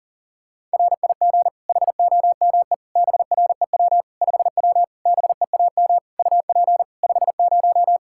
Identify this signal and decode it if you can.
Morse code, heard as a keyed tone, about 40 wpm, 700 Hz. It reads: RIG HOME BREW 5W BEAM UP 50
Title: Practice QSO - Exchange 2 - Info - 40wpm